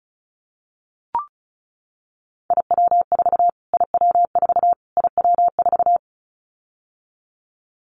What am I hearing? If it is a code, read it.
Morse code, heard as a keyed tone, about 34 wpm, 700 Hz. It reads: IW4 IW4 IW4
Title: Callsign Prefixes for Top 25 DXCC - Repeated 3x - 35wpm